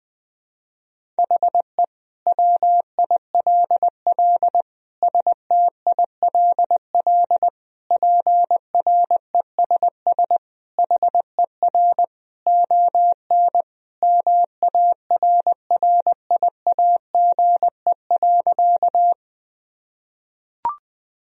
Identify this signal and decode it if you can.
Morse code, heard as a keyed tone, about 20 wpm, 700 Hz. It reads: HE WILL STILL PRESS HER ON MARRIAGE.
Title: Sentences from Top 1000 Words - Review Part 1 - 20wpm